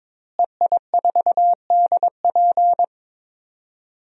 Morse code, heard as a keyed tone, about 22 wpm, 700 Hz. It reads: EI4DP
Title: DXCC-100 Call Signs - 5 Characters - Repeated 2x - 22wpm